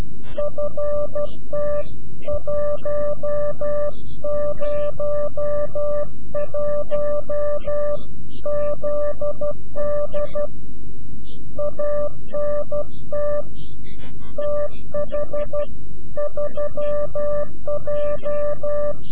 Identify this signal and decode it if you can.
Morse code, heard as a keyed tone, about 12 wpm, 590 Hz. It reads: FT101ZD ANT TH3J
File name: CW-QSO / CQ Call / CQ-QSO